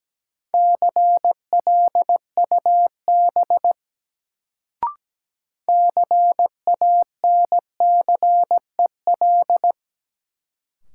Morse code, heard as a keyed tone, about 17 wpm, 700 Hz. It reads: CLUB CANCEL
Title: Sentences from Top 1200 Words - New Word Drill - 17wpm